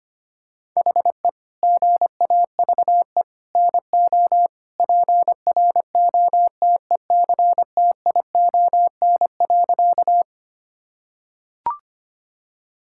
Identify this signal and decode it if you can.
Morse code, heard as a keyed tone, about 25 wpm, 700 Hz. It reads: HE GAVE NO PROTECTION.
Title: Sentences from Top 1600 Words 25wpm